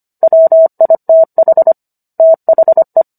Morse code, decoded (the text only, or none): WITH THE